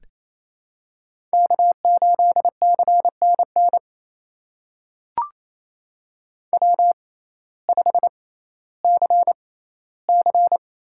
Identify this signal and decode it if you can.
Morse code, heard as a keyed tone, about 28 wpm, spaced as for 15 wpm, 700 Hz. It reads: K8CNN W5CC
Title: USA Call Signs - 6x Character Spacing - 28wpm